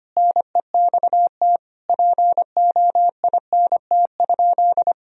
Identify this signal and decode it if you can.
Morse code, heard as a keyed tone, about 25 wpm, 700 Hz. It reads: NEXT POINT?
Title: Sentences from Top 1000 Words - Review Part 1 - 25wpm